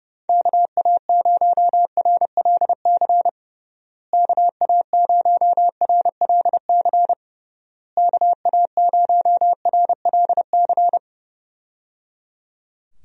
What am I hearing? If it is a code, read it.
Morse code, heard as a keyed tone, about 29 wpm, 700 Hz. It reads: KA0RLC KA0RLC KA0RLC
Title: USA Call Signs - 6 Characters - Repeated 3x - 30wpm